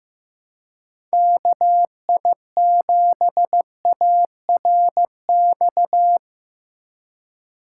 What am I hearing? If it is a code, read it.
Morse code, heard as a keyed tone, about 15 wpm, 700 Hz. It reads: KI7ARX
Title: USA Call Signs - 6 Characters - 4x Character Spacing - 15wpm